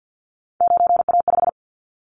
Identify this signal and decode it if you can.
Morse code, heard as a keyed tone, about 50 wpm, 700 Hz. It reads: 9A5
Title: Callsign Prefixes for Top 50 DXCC - 50wpm